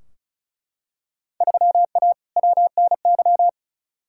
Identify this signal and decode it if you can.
Morse code, heard as a keyed tone, about 34 wpm, 700 Hz. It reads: 3A WNY
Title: Field Day Exchanges - 35wpm